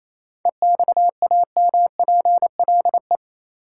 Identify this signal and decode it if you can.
Morse code, heard as a keyed tone, about 28 wpm, 700 Hz. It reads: EXAMPLE
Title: Sets of 2 Words - New Word Drill - 28wpm